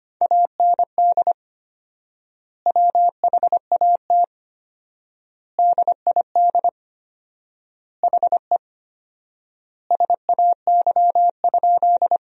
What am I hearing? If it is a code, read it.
Morse code, heard as a keyed tone, about 25 wpm, 700 Hz. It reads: AND WHAT DID HE SAY?
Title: 4x Word-Spacing - Sentences from Top 200 Words - 25wpm